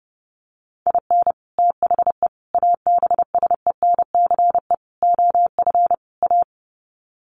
Morse code, heard as a keyed tone, about 30 wpm, 700 Hz. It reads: IN THE ABSENCE OF A
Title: Sets of 5 Words Encore 30wpm